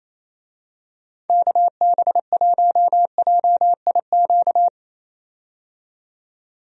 Morse code, heard as a keyed tone, about 27 wpm, 700 Hz. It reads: KB1JIQ
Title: USA Call Signs - 6 Characters - 28wpm